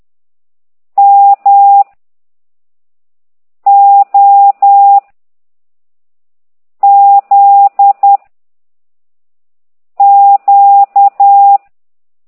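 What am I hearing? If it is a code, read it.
Morse code, heard as a keyed tone, about 10 wpm, 800 Hz. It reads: M O Z Q
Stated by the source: Morse code lesson 5 exercise 10 by CT2IDW, containing only characters M, O, G, Z, Q and comma.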